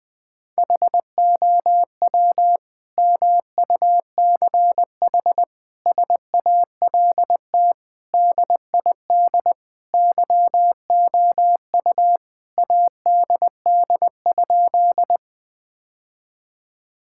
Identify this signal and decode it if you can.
Morse code, heard as a keyed tone, about 20 wpm, 700 Hz. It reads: HOW MUCH SALT DID YOU ADD?
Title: Sentences from Top 1000 Words - Review Part 5 - 20wpm